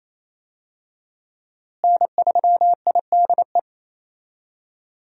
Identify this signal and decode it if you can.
Morse code, heard as a keyed tone, about 28 wpm, 700 Hz. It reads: N3IDE
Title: USA Call Signs - 5 Characters - Repeated 2x - 28wpm